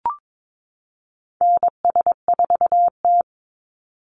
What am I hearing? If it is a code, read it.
Morse code, heard as a keyed tone, about 22 wpm, 690 Hz. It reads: NS4T
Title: Call Signs Repeated 3x - 22wpm